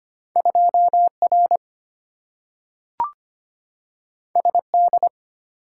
Morse code, heard as a keyed tone, about 25 wpm, 700 Hz. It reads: 2R SD
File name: Two Letter-Number Combinations - 25wpm